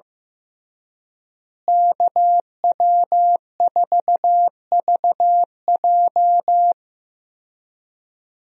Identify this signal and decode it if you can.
Morse code, heard as a keyed tone, about 15 wpm, 700 Hz. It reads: KW4VJ
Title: USA Call Signs - 5 Characters - Repeated 3x - 15wpm